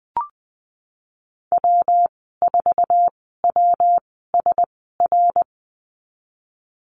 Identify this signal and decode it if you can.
Morse code, heard as a keyed tone, about 20 wpm, spaced as for 14 wpm, 700 Hz. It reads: W4WSR
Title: 128 Lesson 21g - Number 6 - Call Signs 20wpm